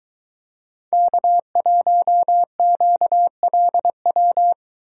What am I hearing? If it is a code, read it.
Morse code, heard as a keyed tone, about 23 wpm, 700 Hz. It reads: K1QLW